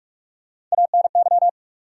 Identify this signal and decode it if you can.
Morse code, heard as a keyed tone, about 43 wpm, 700 Hz. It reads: ANY
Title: Sets of 3 Words - New Word Drill - 45wpm